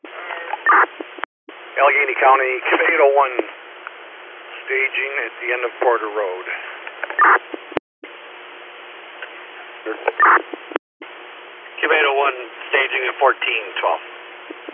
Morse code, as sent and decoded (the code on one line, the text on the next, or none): none